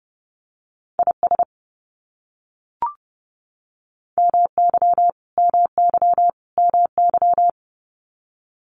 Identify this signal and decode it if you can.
Morse code, heard as a keyed tone, about 30 wpm, 700 Hz. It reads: IS MY MY MY
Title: Top 500 Words Repeated 30wpm